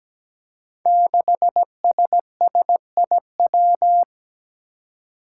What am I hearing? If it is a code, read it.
Morse code, heard as a keyed tone, about 17 wpm, 700 Hz. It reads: 6SSIW
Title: Code Groups - 5 Letter:Numbers - 17wpm